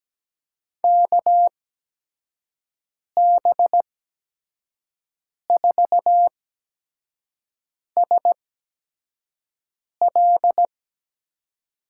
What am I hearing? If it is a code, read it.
Morse code, heard as a keyed tone, about 17 wpm, spaced as for 5 wpm, 700 Hz. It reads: KB4SL
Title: USA Call Signs - 6 Characters - 8x Character Spacing - 17wpm